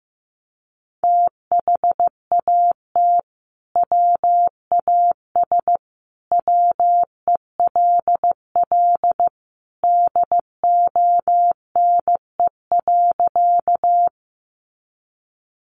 Morse code, heard as a keyed tone, about 15 wpm, 700 Hz. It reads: THAT WAS WELL DONE.
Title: Sentences from Top 500 Words 15wpm